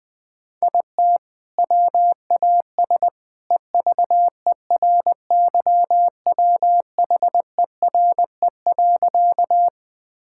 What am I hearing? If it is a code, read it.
Morse code, heard as a keyed tone, about 20 wpm, 700 Hz. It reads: IT WAS EVERYWHERE.
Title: Sentences from Top 1600 Words 20wpm